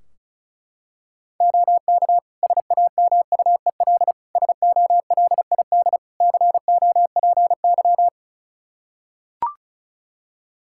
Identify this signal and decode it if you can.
Morse code, heard as a keyed tone, about 34 wpm, 700 Hz. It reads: OK SAMUEL SOLID COPY
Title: K1USN SST QSO Practice 35wpm